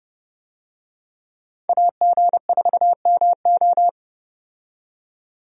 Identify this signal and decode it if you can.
Morse code, heard as a keyed tone, about 30 wpm, 700 Hz. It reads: AG4MO
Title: USA Call Signs - 5 Characters - 30wpm